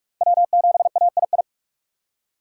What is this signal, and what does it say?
Morse code, heard as a keyed tone, about 44 wpm, 690 Hz. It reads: W7AII